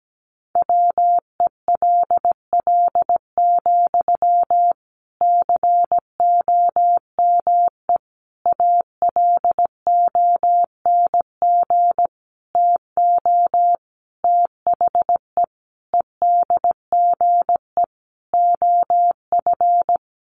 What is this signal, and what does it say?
Morse code, heard as a keyed tone, about 17 wpm, 700 Hz. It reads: WELL, COME ALONG TO THE EDGE OF
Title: Sentences from Top 1300 Words 17wpm